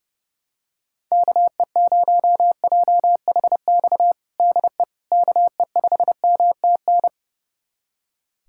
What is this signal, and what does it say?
Morse code, heard as a keyed tone, about 30 wpm, 700 Hz. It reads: KE0JHX DE KE5MTN
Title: USA Callsign DE Callsign - 6 Characters - 30wpm